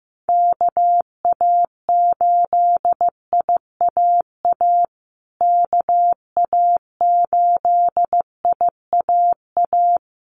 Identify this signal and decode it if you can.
Morse code, heard as a keyed tone, about 15 wpm, 700 Hz. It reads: KA8IAA KA8IAA
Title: USA Call Signs Repeated 3x 15wpm